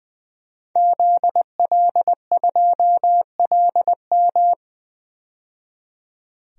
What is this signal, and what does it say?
Morse code, heard as a keyed tone, about 20 wpm, 700 Hz. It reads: ZL2LM
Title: DXCC-100 Call Signs - 5 Characters - 20wpm